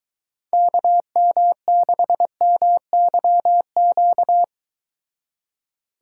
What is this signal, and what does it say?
Morse code, heard as a keyed tone, about 23 wpm, 700 Hz. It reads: KM6MYQ